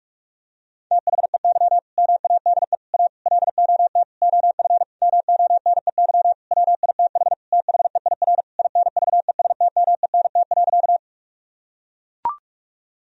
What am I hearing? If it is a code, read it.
Morse code, heard as a keyed tone, about 42 wpm, 700 Hz. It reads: THEY MADE A LOT OF MONEY WITH THEIR INVESTMENT.
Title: Sentences from Top 1000 Words - Review Part 4 - 45wpm